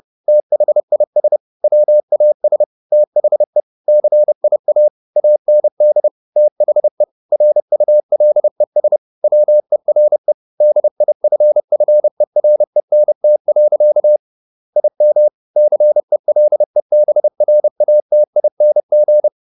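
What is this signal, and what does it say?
Morse code, heard as a keyed tone, about 30 wpm, 600 Hz. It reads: THIS WAS THE CIA AND THE RULES WERE DIFFERENT. IM CELEBRATING
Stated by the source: Morse code at 30 wpm, the novel Never by Ken Follett in English, from CW med Gnister.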